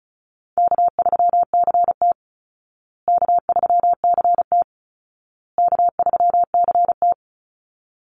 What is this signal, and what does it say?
Morse code, heard as a keyed tone, about 34 wpm, 700 Hz. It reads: K3CT K3CT K3CT
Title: Easy Call Signs Repeated 3x - 35wpm